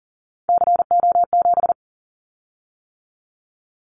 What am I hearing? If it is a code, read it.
Morse code, heard as a keyed tone, about 40 wpm, 700 Hz. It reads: CO7
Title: Callsign Prefixes for Top 100 DXCC - 40wpm